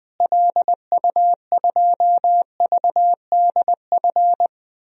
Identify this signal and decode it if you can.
Morse code, heard as a keyed tone, about 21 wpm, 700 Hz. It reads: LU2VDF
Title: DXCC-100 Call Signs - 6 Characters - Repeated 3x - 20wpm